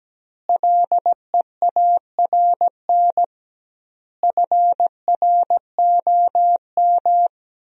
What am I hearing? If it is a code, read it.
Morse code, heard as a keyed tone, about 17 wpm, 700 Hz. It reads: LEARN FROM
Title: 2x Word-Spacing - Sentences from Top 300 Words - 17wpm